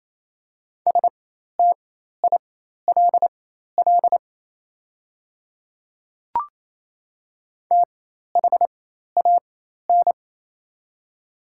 Morse code, heard as a keyed tone, about 28 wpm, spaced as for 13 wpm, 700 Hz. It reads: STILL THAN
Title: 4x Character Spacing - Top 300 Words - 28wpm